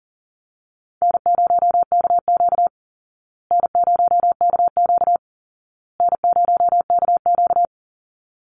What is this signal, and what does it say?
Morse code, heard as a keyed tone, about 40 wpm, 700 Hz. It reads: N0KQ N0KQ N0KQ
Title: Easy Call Signs Repeated 3x - 40wpm